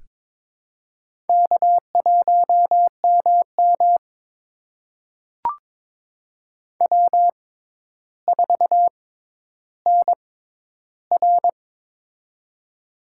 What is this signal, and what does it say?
Morse code, heard as a keyed tone, about 23 wpm, spaced as for 11 wpm, 700 Hz. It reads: K1MM W4NR
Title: USA Call Signs - 4 Characters - 6x Character Spacing - 22wpm